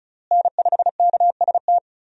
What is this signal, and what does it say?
Morse code, heard as a keyed tone, about 34 wpm, 700 Hz. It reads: N5KST
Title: USA Call Signs - 5 Characters - Repeated 2x - 35wpm